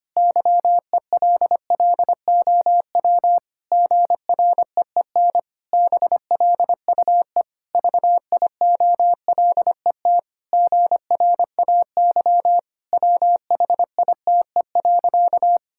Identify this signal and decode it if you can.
Morse code, heard as a keyed tone, about 25 wpm, 700 Hz. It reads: YELLOW GREEN BLUE VIOLET GRAY WHITE.
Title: Pangram Challenge v2 - 25wpm